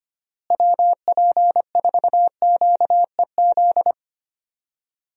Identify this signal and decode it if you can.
Morse code, heard as a keyed tone, about 25 wpm, 700 Hz. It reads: WP4QEZ
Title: USA Call Signs - 6 Characters - Repeated 2x - 25wpm